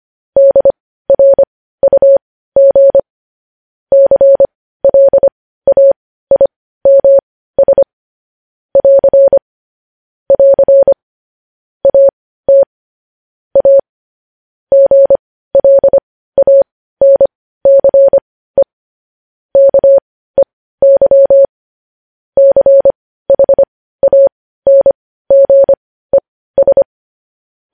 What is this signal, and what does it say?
Morse code, heard as a keyed tone, about 25 wpm, spaced as for 15 wpm, 550 Hz. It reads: DRUG CLAIMS <AR> <AR> AT A GLANCE KEY CHANGES
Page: news headlines in morse code at 15 wpm